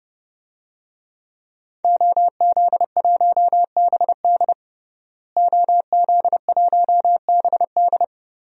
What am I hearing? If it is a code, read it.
Morse code, heard as a keyed tone, about 30 wpm, 700 Hz. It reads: OZ1BD OZ1BD
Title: DXCC-100 Call Signs - 5 Characters - Repeated 2x - 30wpm